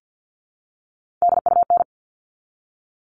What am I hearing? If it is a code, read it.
Morse code, heard as a keyed tone, about 50 wpm, 700 Hz. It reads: DUN
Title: All Cut Numbers - 3-Digit 50wpm